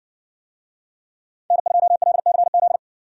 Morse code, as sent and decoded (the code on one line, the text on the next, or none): -. ..--- .-.. -.-. --..
N2LCZ